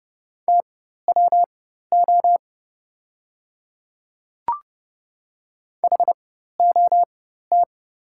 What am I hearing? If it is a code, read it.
Morse code, heard as a keyed tone, about 30 wpm, spaced as for 14 wpm, 690 Hz. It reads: TWO HOT
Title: 4x Character Spacing - Top 100 Words - 30wpm